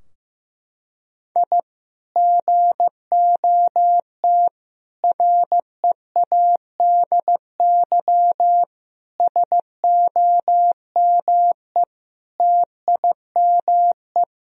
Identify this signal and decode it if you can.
Morse code, heard as a keyed tone, about 15 wpm, 700 Hz. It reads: I GOT READY SOME TIME
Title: Sentences from Top 1000 Words - Review Part 3 - 15wpm